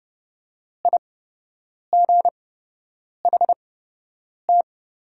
Morse code, heard as a keyed tone, about 30 wpm, spaced as for 8 wpm, 700 Hz. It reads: IGHT
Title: 8x Character Spacing - Top 300 Words - 30wpm